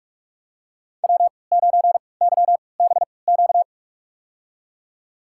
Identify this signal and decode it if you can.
Morse code, heard as a keyed tone, about 46 wpm, spaced as for 25 wpm, 700 Hz. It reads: W9YBQ
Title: USA Call Signs - 5 Characters - 3x Character Spacing - 45wpm